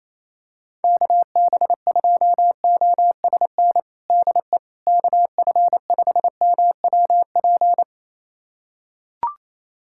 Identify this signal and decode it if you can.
Morse code, heard as a keyed tone, about 28 wpm, 700 Hz. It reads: KB2OSN DE KF5MWP
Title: USA Callsign DE Callsign 28wpm